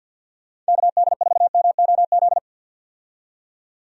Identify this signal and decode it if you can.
Morse code, heard as a keyed tone, about 51 wpm, 700 Hz. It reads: KD4MOZ
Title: USA Call Signs - 6 Characters - 50wpm